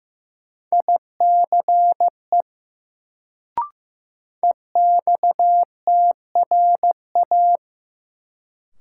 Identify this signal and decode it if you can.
Morse code, heard as a keyed tone, about 15 wpm, 700 Hz. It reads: ICE EXTRA
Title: Sentences from Top 1000 Words - New Word Drill - 15wpm